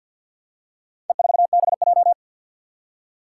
Morse code, heard as a keyed tone, about 52 wpm, 700 Hz. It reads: E4BJ
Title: Four Challenging Letter-Number Combinations 50wpm